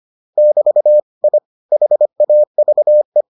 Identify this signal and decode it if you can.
Morse code, heard as a keyed tone, about 25 wpm, 600 Hz. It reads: <BT> I HAVE